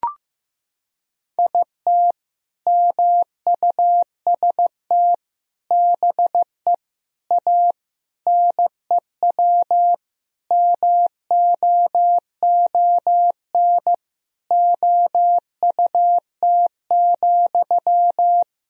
Morse code, heard as a keyed tone, about 15 wpm, 700 Hz. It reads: IT MUST BE A NEW MOON OUT,
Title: Sentences from Top 500 Words and Abbreviations 15wpm